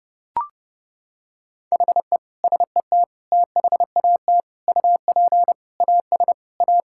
Morse code, heard as a keyed tone, about 30 wpm, 700 Hz. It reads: HE SET THAT UP AS A
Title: Sentences from Top 1000 Words - Review Part 6 - 30wpm